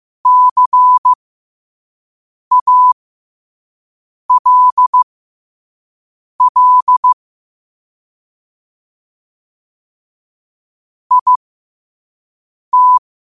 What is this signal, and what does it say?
Morse code, heard as a keyed tone, about 15 wpm, spaced as for 5 wpm, 1000 Hz. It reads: CALL IT
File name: qotdfiles / 5wpm